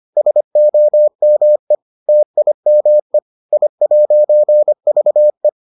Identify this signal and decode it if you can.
Morse code, heard as a keyed tone, about 25 wpm, 600 Hz. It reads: SOME TIME I'VE